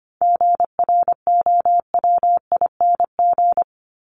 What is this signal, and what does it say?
Morse code, heard as a keyed tone, about 25 wpm, 700 Hz. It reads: GROWING
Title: ING Suffix - Top 100 Words with Base Words 25wpm